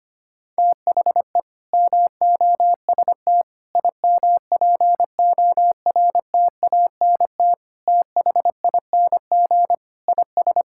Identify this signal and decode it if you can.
Morse code, heard as a keyed tone, about 25 wpm, 700 Hz. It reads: THE MOST IMPORTANT THING IS
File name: practice / 25wpm / Sentences with 3-Word Phrases - 25wpm